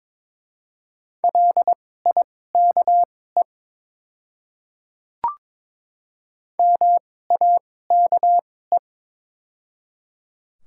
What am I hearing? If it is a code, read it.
Morse code, heard as a keyed tone, about 21 wpm, spaced as for 16 wpm, 700 Hz. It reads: LIKE MAKE
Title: 2x Character Spacing - Top 100 Words - 22wpm